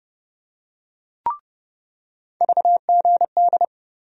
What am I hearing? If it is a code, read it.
Morse code, heard as a keyed tone, about 30 wpm, 700 Hz. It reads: VGD
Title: USA Call Sign Postfixes - 3 Letters 30wpm